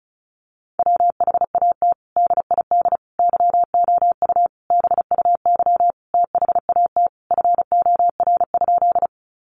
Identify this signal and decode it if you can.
Morse code, heard as a keyed tone, about 34 wpm, 700 Hz. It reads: WHAT DID YOU BUY THAT FOR?
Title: Sentences from Top 700 Words 35wpm